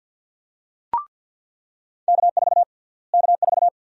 Morse code, heard as a keyed tone, about 47 wpm, 700 Hz. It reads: K4 K4
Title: USA Call Sign Prefixes - Repeated 2x - 50wpm